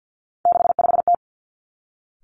Morse code, heard as a keyed tone, about 50 wpm, 700 Hz. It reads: 65T
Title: Three Challenging Letter-Number Combinations 50wpm